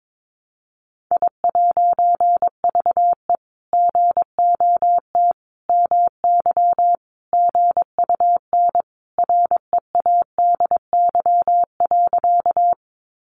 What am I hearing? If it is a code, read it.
Morse code, heard as a keyed tone, about 22 wpm, 700 Hz. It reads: I'VE GOT MY GUN READY.
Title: A Sound Of Thunder 22wpm